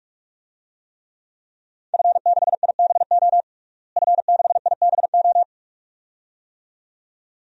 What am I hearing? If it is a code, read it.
Morse code, heard as a keyed tone, about 46 wpm, 700 Hz. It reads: F6IBO F6IBO
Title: DXCC-100 Call Signs - 5 Characters - Repeated 2x - 45wpm